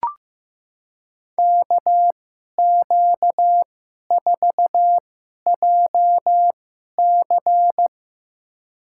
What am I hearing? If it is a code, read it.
Morse code, heard as a keyed tone, about 15 wpm, spaced as for 11 wpm, 700 Hz. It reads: KQ4JC
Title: USA Call Signs - 5 Characters - 2x Character Spacing - 15wpm